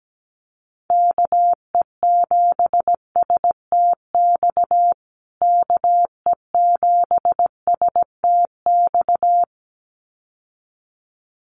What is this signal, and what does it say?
Morse code, heard as a keyed tone, about 17 wpm, 700 Hz. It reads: KE7STX KE7STX
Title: USA Call Signs Repeated 2x 17wpm